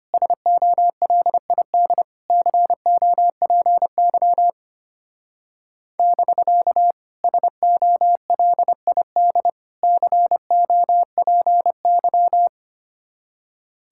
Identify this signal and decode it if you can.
Morse code, heard as a keyed tone, about 31 wpm, 690 Hz. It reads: SOLID COPY <BK> SOLID COPY
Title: Speed-Racing Common QSO Phrases - 25wpm